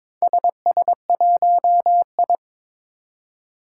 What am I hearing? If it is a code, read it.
Morse code, heard as a keyed tone, about 22 wpm, 700 Hz. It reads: SS1I
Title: Four Challenging Letter-Number Combinations 22wpm